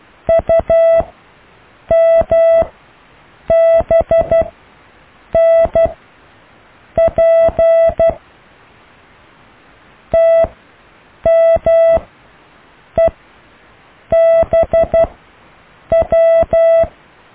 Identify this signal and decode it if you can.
Morse code, heard as a keyed tone, about 12 wpm, spaced as for 7 wpm, 660 Hz. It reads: UMBNP TMEBW
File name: cw / IJLUM+CDRO+EKTPV+BFHNW